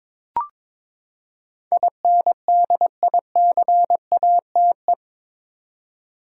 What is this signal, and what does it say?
Morse code, heard as a keyed tone, about 22 wpm, 700 Hz. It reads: INDICATE
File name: practice / 22wpm / Eight Letter Words - 22wpm